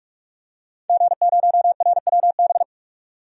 Morse code, heard as a keyed tone, about 43 wpm, 700 Hz. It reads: G0RWB
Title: DXCC-100 Call Signs - 5 Characters - 45wpm